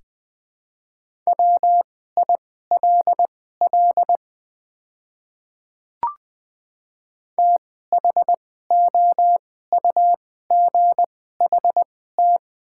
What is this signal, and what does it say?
Morse code, heard as a keyed tone, about 21 wpm, spaced as for 15 wpm, 700 Hz. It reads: WILL THOUGHT
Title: 2x Character Spacing - Top 300 Words - 20wpm